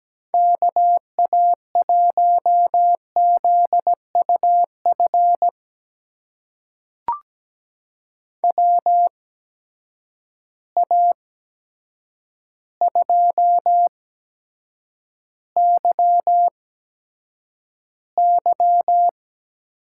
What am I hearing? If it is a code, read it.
Morse code, heard as a keyed tone, about 17 wpm, spaced as for 8 wpm, 700 Hz. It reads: KA1ZUF WA2YY
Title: USA Call Signs - 6 Characters - 8x Character Spacing - 17wpm